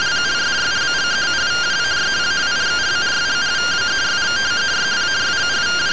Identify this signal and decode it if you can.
Two-tone teleprinter signal, about 170 Hz shift.